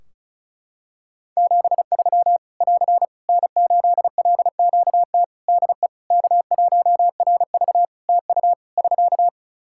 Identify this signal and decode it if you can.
Morse code, heard as a keyed tone, about 36 wpm, 700 Hz. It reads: 73 <AR> N8LQT DE K1RV TU <SK>
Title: K1USN SST QSO Practice 35wpm